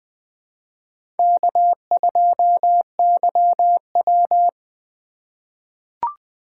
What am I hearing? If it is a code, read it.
Morse code, heard as a keyed tone, about 21 wpm, 700 Hz. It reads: K2YW
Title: USA Call Signs - 4 Characters - 3x Character Spacing - 20wpm